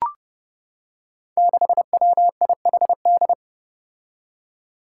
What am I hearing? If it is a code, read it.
Morse code, heard as a keyed tone, about 30 wpm, 700 Hz. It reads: 6WIHD
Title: Five Challenging Letter-Number Combinations 30wpm